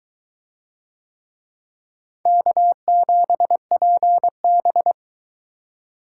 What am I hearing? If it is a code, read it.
Morse code, heard as a keyed tone, about 22 wpm, 700 Hz. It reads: K7PB